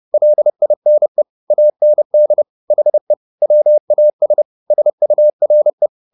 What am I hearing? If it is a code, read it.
Morse code, heard as a keyed tone, about 30 wpm, 600 Hz. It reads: LINE AND HE WAS SURE